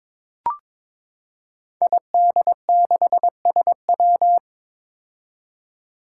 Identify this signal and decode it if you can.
Morse code, heard as a keyed tone, about 22 wpm, 700 Hz. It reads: ID6SW
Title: Five Challenging Letter-Number Combinations 22wpm